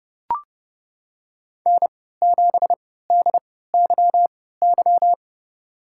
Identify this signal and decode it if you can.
Morse code, heard as a keyed tone, about 30 wpm, spaced as for 17 wpm, 700 Hz. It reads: N7DYY
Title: USA Call Signs - 5 Characters - 3x Character Spacing - 30wpm